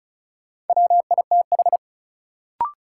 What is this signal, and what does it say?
Morse code, heard as a keyed tone, about 34 wpm, 700 Hz. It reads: WITH
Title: Sentences from Top 100 Words - New Word Drill - 35wpm